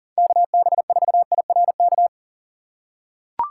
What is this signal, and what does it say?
Morse code, heard as a keyed tone, about 40 wpm, 700 Hz. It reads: KB4IRK
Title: USA Call Signs - 6 Characters - 6x Character Spacing - 40wpm